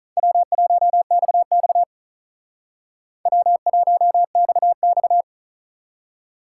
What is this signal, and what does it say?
Morse code, heard as a keyed tone, about 44 wpm, 700 Hz. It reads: W1XX W1XX